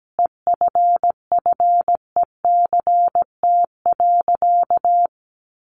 Morse code, heard as a keyed tone, about 17 wpm, 700 Hz. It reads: EFFECT.
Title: Sentences from Top 1400 Words 17wpm